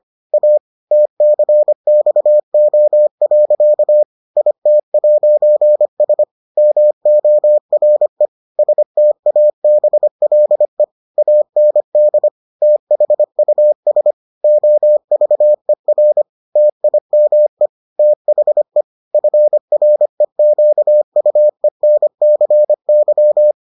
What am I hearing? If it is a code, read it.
Morse code, heard as a keyed tone, about 25 wpm, 600 Hz. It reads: A TCXO. IT'S MORE STABLE AND THUS OVER TIME THE FREQUENCY